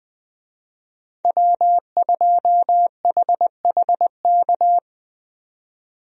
Morse code, heard as a keyed tone, about 20 wpm, 700 Hz. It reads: W2HHK
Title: USA Call Signs - 5 Characters - 20wpm